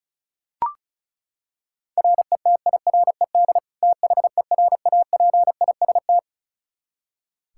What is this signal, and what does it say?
Morse code, heard as a keyed tone, about 34 wpm, 700 Hz. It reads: RETIRED THERAPIST
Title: Practice QSO - Exchange 3 - More Info - 35wpm